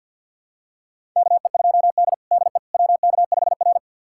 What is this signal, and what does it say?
Morse code, heard as a keyed tone, about 53 wpm, 700 Hz. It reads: KE2D DE WK5R
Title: USA Callsign DE Callsign - 4 Characters - 50wpm